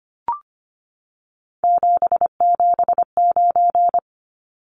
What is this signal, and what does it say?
Morse code, heard as a keyed tone, about 25 wpm, 700 Hz. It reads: 779
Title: Numbers - 3-Digit 25wpm